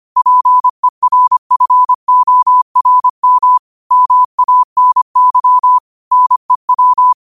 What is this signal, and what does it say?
Morse code, heard as a keyed tone, about 25 wpm, 1000 Hz. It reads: PERFORM MANY NEW
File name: 25wpm